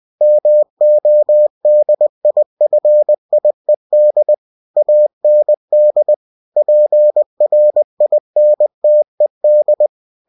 Morse code, heard as a keyed tone, about 20 wpm, 600 Hz. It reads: MODIFIED AND PRINTED
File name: cw-books / winnie-the-pooh / winnie-20wpm-page-0112